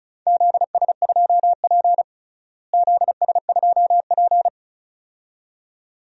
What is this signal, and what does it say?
Morse code, heard as a keyed tone, about 34 wpm, 700 Hz. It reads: ZS2P ZS2P
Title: DXCC-100 Call Signs - 4 Characters - Repeated 2x - 35wpm